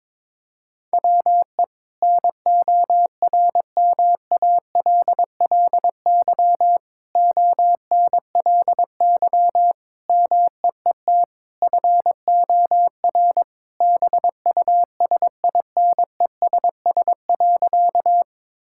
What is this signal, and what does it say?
Morse code, heard as a keyed tone, about 22 wpm, 700 Hz. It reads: WE NORMALLY ONLY MEET FOR BUSINESS.
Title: Sentences from Top 1000 Words - Review Part 6 - 22wpm